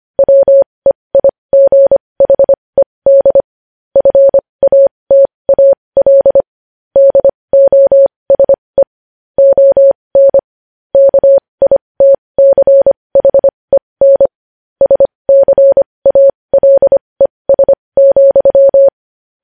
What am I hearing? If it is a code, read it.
Morse code, heard as a keyed tone, about 25 wpm, spaced as for 20 wpm, 550 Hz. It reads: WEIGHED FATAL DOSE ON KITCHEN SCALES,
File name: podcast / News headlines-1x-25x20